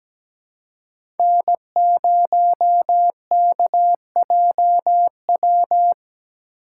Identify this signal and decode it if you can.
Morse code, heard as a keyed tone, about 17 wpm, 700 Hz. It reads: N0KJW
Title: USA Call Signs - 5 Characters - 6x Character Spacing - 17wpm